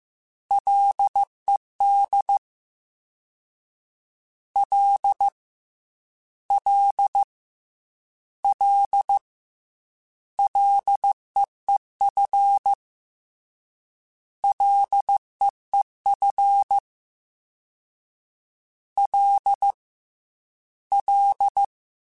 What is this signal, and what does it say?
Morse code, heard as a keyed tone, about 15 wpm, 800 Hz. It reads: LED L L L LEEF LEEF L L
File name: F and L / F and L mp3 / F and L Prefixes 3x2 Twice @13